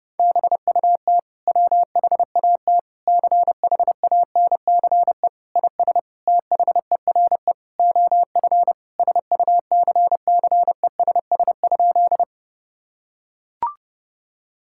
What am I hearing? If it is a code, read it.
Morse code, heard as a keyed tone, about 30 wpm, 700 Hz. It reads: BUT WHAT CHANCE IS THERE OF SUCCESS?
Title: Sentences from Top 1000 Words - Review Part 6 - 30wpm